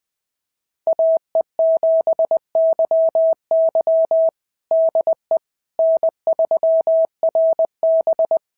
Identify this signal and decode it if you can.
Morse code, heard as a keyed tone, about 20 wpm, 660 Hz. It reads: AE7YY DE N3RB
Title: SOTA QSO Practice 20wpm